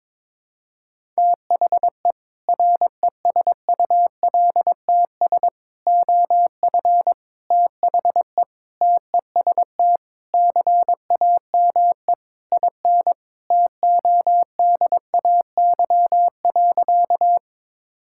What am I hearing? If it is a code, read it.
Morse code, heard as a keyed tone, about 22 wpm, 700 Hz. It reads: THE RESULTS OF THE TEST CAME IN TODAY.
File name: practice / 22wpm / Sentences with 4-Word Phrases - 22wpm